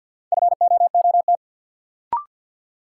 Morse code, heard as a keyed tone, about 52 wpm, 700 Hz. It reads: FOOT
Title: Sentences from Top 500 Words - New Word Drill - 50wpm